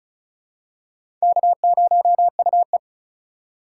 Morse code, heard as a keyed tone, about 36 wpm, 700 Hz. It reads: K0UE
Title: USA Call Signs - 4 Characters - 4x Character Spacing - 35wpm